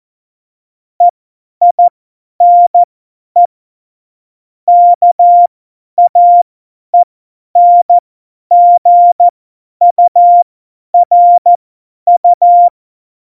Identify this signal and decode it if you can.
Morse code, heard as a keyed tone, about 14 wpm, spaced as for 10 wpm, 700 Hz. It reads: EINE KAENGURU
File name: Morsekurs / Witze / Witz 18. September 2018, 50 F70